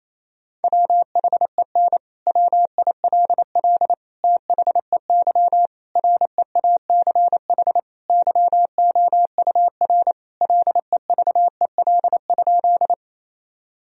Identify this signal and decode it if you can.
Morse code, heard as a keyed tone, about 28 wpm, 700 Hz. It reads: WHEN WILL THEY REACH YOUR LEVEL?
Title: Sentences from Top 1000 Words - Review Part 4 - 28wpm